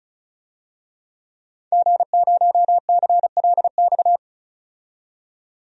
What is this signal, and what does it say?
Morse code, heard as a keyed tone, about 36 wpm, 700 Hz. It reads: G0CLX
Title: DXCC-100 Call Signs - 5 Characters - 35wpm